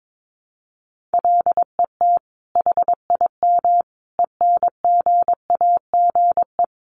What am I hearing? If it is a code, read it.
Morse code, heard as a keyed tone, about 22 wpm, 700 Hz. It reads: LET HIM ENGAGE
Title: Sentences from Top 1500 Words 22wpm